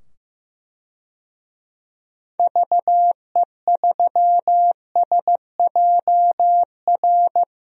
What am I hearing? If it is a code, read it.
Morse code, heard as a keyed tone, about 15 wpm, 700 Hz. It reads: VE3SJR
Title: DXCC-100 Call Signs - 6 Characters - 15wpm